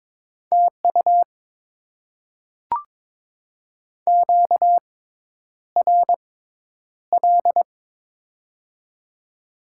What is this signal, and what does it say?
Morse code, heard as a keyed tone, about 22 wpm, spaced as for 10 wpm, 690 Hz. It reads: TU QRL
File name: Top 20 QSO Elements - 6x Character Spacing - 22wpm